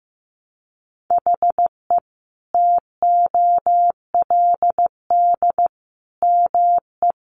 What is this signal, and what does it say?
Morse code, heard as a keyed tone, about 15 wpm, 700 Hz. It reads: HE TOLD ME
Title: Sentences from Top 1300 Words 15wpm